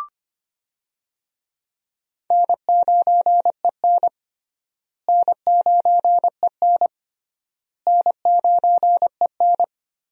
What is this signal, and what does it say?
Morse code, heard as a keyed tone, about 25 wpm, 700 Hz. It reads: N9EN N9EN N9EN
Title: USA Call Signs - 4 Characters - Repeated 3x - 25wpm